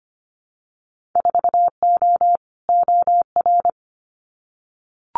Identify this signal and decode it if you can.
Morse code, heard as a keyed tone, about 25 wpm, 700 Hz. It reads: 4O OR
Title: Winter Field Day Exchanges Encore 25wpm